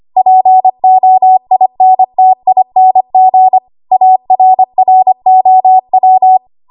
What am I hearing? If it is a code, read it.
Morse code, heard as a keyed tone, about 25 wpm, 750 Hz. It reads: POINTING ARROW
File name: Archive / 25 WPM / 25WPM